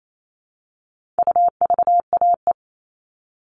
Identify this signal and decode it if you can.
Morse code, heard as a keyed tone, about 28 wpm, 700 Hz. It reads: UVAE
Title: All Cut Numbers - 4-Digit 28wpm